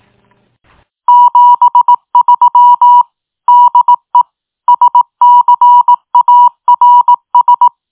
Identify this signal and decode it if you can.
Morse code, heard as a keyed tone, about 18 wpm, 970 Hz. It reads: 73 DE SCARS
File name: tin-upgrade / QSO 2022 09 15 .mp3